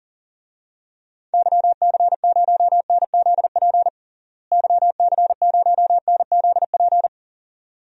Morse code, heard as a keyed tone, about 40 wpm, 700 Hz. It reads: YC0NZP YC0NZP
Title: DXCC-100 Call Signs - 6 Characters - Repeated 2x - 40wpm